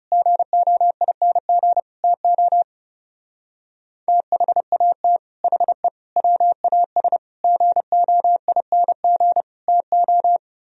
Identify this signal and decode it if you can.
Morse code, heard as a keyed tone, about 37 wpm, 700 Hz. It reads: GOING TO THAT HE WAS GOING TO
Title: Speed-Racing - Sets of 5 Words 30wpm